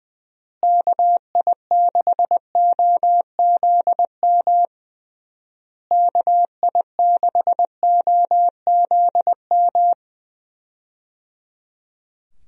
Morse code, heard as a keyed tone, about 20 wpm, 700 Hz. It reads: KI6OZM KI6OZM
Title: USA Call Signs - 6 Characters - Repeated 3x - 20wpm